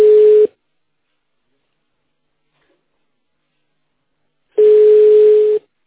Ringback tone (Europe, 425 Hz).